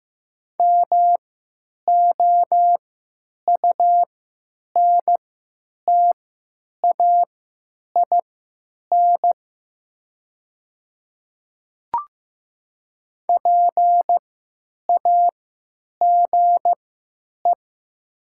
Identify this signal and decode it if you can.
Morse code, heard as a keyed tone, about 15 wpm, spaced as for 9 wpm, 700 Hz. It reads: MOUNTAIN PAGE
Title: 3x Character Spacing - Top 500 Words - 15wpm